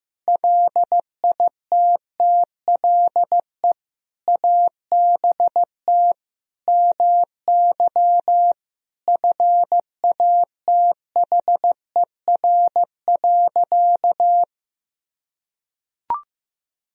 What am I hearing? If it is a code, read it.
Morse code, heard as a keyed tone, about 15 wpm, 700 Hz. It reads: LITTLE ABT MY FATHER.
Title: Sentences from Top 400 Words and Abbreviations 15wpm